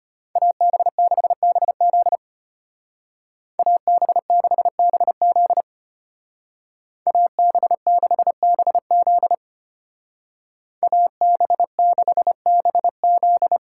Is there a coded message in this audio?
Yes: Morse code from 0.4 to 13.6 s, heard as a keyed tone, about 34 wpm, 700 Hz. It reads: AB6BZ AB6BZ AB6BZ AB6BZ